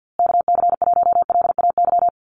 Morse code, heard as a keyed tone, about 50 wpm, 700 Hz. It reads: KC1LAY
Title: USA Call Signs Repeated 2x 50wpm